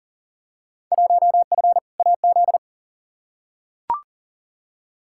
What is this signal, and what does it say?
Morse code, heard as a keyed tone, about 38 wpm, 700 Hz. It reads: 1F AZ
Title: Field Day Exchanges - 40wpm